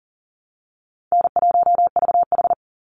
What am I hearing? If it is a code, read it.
Morse code, heard as a keyed tone, about 38 wpm, 700 Hz. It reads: N1VH
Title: Easy Call Signs Repeated 2x - 40wpm